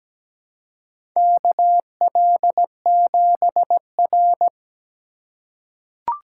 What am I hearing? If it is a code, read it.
Morse code, heard as a keyed tone, about 17 wpm, 700 Hz. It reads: KL7R
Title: USA Call Signs - 4 Characters - 3x Character Spacing - 17wpm